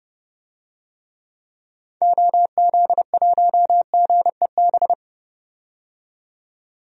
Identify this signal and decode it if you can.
Morse code, heard as a keyed tone, about 30 wpm, 700 Hz. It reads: OZ1GEB
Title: DXCC-100 Call Signs - 6 Characters - 30wpm